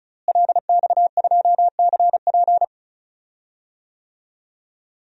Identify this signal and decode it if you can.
Morse code, heard as a keyed tone, about 34 wpm, 700 Hz. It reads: LX2CP
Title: DXCC-100 Call Signs - 5 Characters - Repeated 2x - 35wpm